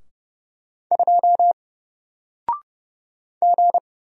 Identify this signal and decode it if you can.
Morse code, heard as a keyed tone, about 30 wpm, 700 Hz. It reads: 2 G
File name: Single-Letter-Number-30wpm